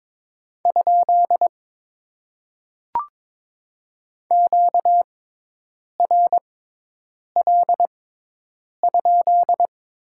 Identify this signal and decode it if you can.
Morse code, heard as a keyed tone, about 22 wpm, spaced as for 8 wpm, 700 Hz. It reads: ? QRL?
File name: Top 90 QSO Elements - 6x Character Spacing - 22wpm